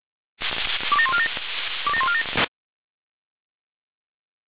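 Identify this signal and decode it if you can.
Five-tone selective call (ZVEI), heard as quick runs of five tones, tones of about 70 ms. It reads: 27147 27147
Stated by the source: ZVEI five-tone selective call received on an SDR.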